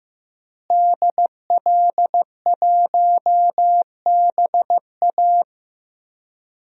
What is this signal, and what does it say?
Morse code, heard as a keyed tone, about 15 wpm, 700 Hz. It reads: DL1BA
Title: DXCC-100 Call Signs - 5 Characters - Repeated 2x - 15wpm